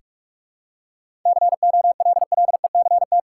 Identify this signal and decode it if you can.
Morse code, heard as a keyed tone, about 46 wpm, 700 Hz. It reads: COLLECT
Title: Sentences from Top 1600 Words 45wpm